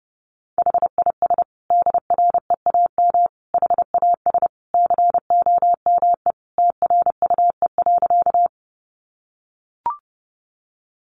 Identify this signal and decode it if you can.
Morse code, heard as a keyed tone, about 30 wpm, 700 Hz. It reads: HIS DREAM HAS COME TRUE.
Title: Sentences from Top 1500 Words 30wpm